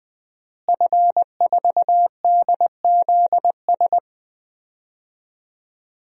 Morse code, heard as a keyed tone, about 20 wpm, 700 Hz. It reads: F4DZS
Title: DXCC-100 Call Signs - 5 Characters - Repeated 3x - 20wpm